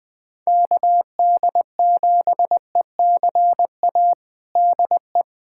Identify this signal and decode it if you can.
Morse code, heard as a keyed tone, about 20 wpm, 700 Hz. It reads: KD7ECA DE
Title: USA Callsign DE Callsign - 6 Characters - 20wpm